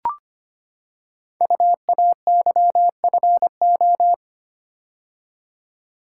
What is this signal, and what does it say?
Morse code, heard as a keyed tone, about 25 wpm, 700 Hz. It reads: UAYFO
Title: Code Groups - 5 Letters - 25wpm